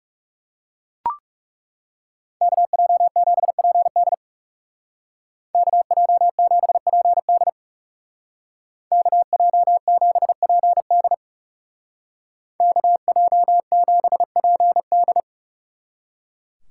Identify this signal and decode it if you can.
Morse code, heard as a keyed tone, about 39 wpm, 700 Hz. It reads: KJ7PD KJ7PD KJ7PD KJ7PD